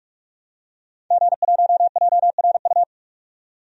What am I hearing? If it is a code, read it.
Morse code, heard as a keyed tone, about 42 wpm, 700 Hz. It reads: G1JRU
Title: DXCC-100 Call Signs - 5 Characters - 45wpm